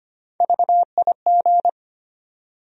Morse code, heard as a keyed tone, about 25 wpm, 700 Hz. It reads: VIG
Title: USA Call Sign Postfixes - 3 Letters 25wpm